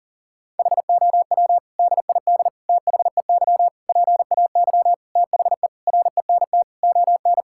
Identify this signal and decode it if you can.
Morse code, heard as a keyed tone, about 37 wpm, 700 Hz. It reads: HOW DID THEY PAY THE RENT ON
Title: Sentences from Top 1000 Words - Review Part 5 - 40wpm